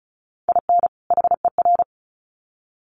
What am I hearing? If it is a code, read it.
Morse code, heard as a keyed tone, about 34 wpm, 700 Hz. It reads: IN HER
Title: Sets of 2 Words 35wpm